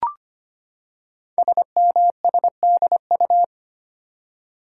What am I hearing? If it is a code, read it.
Morse code, heard as a keyed tone, about 25 wpm, 700 Hz. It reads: SMSDU
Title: Code Groups - 5 Letters - 25wpm